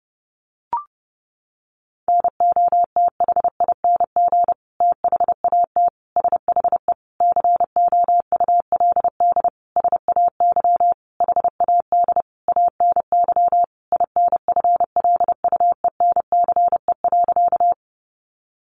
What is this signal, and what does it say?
Morse code, heard as a keyed tone, about 30 wpm, 700 Hz. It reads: NOTHING THAT SHE COULD SAY HAD ANY INFLUENCE.
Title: Sentences from Top 1100 Words 30wpm